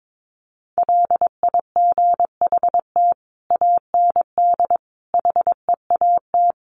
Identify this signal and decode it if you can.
Morse code, heard as a keyed tone, about 22 wpm, 700 Hz. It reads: LIGHT AND HEAT
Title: Sentences from Top 1300 Words 22wpm